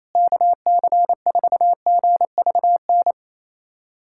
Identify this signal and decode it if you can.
Morse code, heard as a keyed tone, about 28 wpm, 700 Hz. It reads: KC4GVN
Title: USA Callsign DE Callsign - 6 Characters - 28wpm